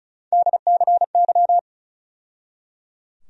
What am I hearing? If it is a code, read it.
Morse code, heard as a keyed tone, about 34 wpm, 700 Hz. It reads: DCY
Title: USA Call Sign Postfixes - 3 Letters 35wpm